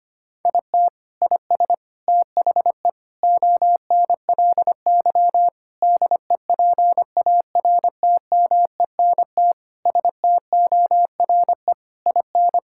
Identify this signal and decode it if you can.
Morse code, heard as a keyed tone, about 25 wpm, 700 Hz. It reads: IT IS THE ONLY DEPARTMENT STORE IN
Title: Sentences from Top 1000 Words - Review Part 5 - 25wpm